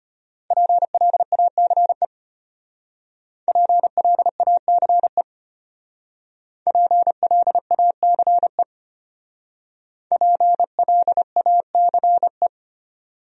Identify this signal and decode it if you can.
Morse code, heard as a keyed tone, about 32 wpm, 700 Hz. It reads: PLACE PLACE PLACE PLACE